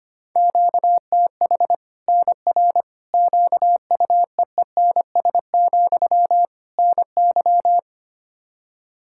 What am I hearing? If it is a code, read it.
Morse code, heard as a keyed tone, about 25 wpm, 700 Hz. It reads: QTH NR QUEENS, NY
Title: Practice QSO - Exchange 1 - Beginning - 25wpm